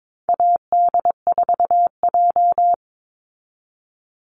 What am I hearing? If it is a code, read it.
Morse code, heard as a keyed tone, about 22 wpm, 700 Hz. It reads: AD4J
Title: Call Signs - 22wpm